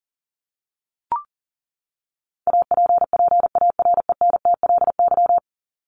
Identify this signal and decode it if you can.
Morse code, heard as a keyed tone, about 40 wpm, 700 Hz. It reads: APPARENTLY
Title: LY Suffix - Top 100 Words with Base Words 40wpm